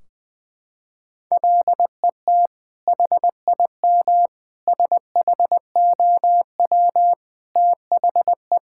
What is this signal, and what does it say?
Morse code, heard as a keyed tone, about 20 wpm, 700 Hz. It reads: LET HIM SHOW THE
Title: Sentences from Top 1600 Words 20wpm